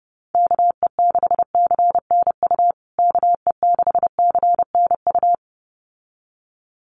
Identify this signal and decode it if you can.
Morse code, heard as a keyed tone, about 30 wpm, 700 Hz. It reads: KE6CNU KE6CNU
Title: USA Call Signs Repeated 2x 30wpm